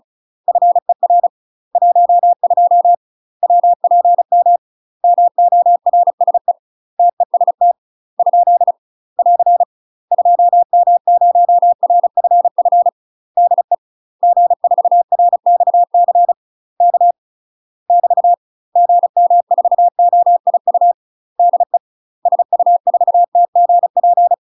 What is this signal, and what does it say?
Morse code, heard as a keyed tone, about 34 wpm, 700 Hz. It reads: FER 12 WPM MORSE TEST ? <AR> 2M0RFF DE G4RXC K <BT> GM4OIU DE SU4TGP
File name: UK MASTER QSOs TRAINING / 35WPM / UK MASTER QSOs 0043